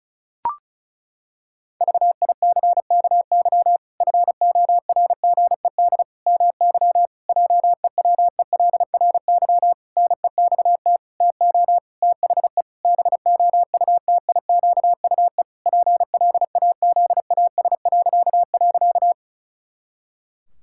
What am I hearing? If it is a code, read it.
Morse code, heard as a keyed tone, about 34 wpm, 700 Hz. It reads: VICKY FORGED MY JEWELRY NEXT TO THE BOUTIQUE PLAZAS..
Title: Pangram Challenge v2 - 35wpm